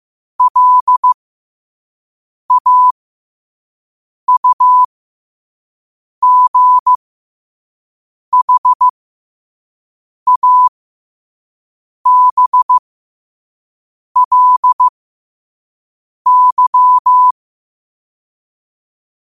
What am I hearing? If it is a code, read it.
Morse code, heard as a keyed tone, about 15 wpm, spaced as for 5 wpm, 1000 Hz. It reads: LAUGHABLY